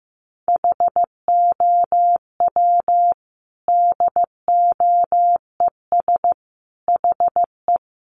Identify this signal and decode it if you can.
Morse code, heard as a keyed tone, about 15 wpm, 700 Hz. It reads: HOW DOES HE
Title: Sentences from Top 1300 Words 15wpm